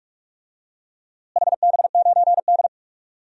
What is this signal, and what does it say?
Morse code, heard as a keyed tone, about 46 wpm, 700 Hz. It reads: HB9D
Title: DXCC-100 Call Signs - 4 Characters - Repeated 2x - 45wpm